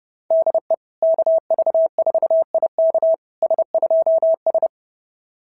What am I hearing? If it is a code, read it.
Morse code, heard as a keyed tone, about 30 wpm, 690 Hz. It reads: DE KV4IK S2S
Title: SOTA QSO Practice 30wpm